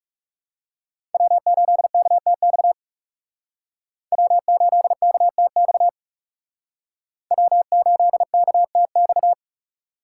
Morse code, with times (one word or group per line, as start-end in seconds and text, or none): 1.1-2.7 W8KTX
4.1-5.9 W8KTX
7.3-9.3 W8KTX